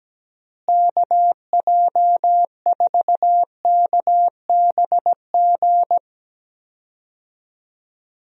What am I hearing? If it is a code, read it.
Morse code, heard as a keyed tone, about 17 wpm, 700 Hz. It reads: KJ4KBG
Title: USA Call Signs - 6 Characters - Repeated 2x - 17wpm